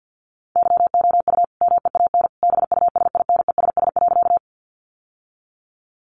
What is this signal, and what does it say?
Morse code, heard as a keyed tone, about 50 wpm, 700 Hz. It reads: YOU MEAN BUSINESS.
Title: Sentences from Top 600 Words 50wpm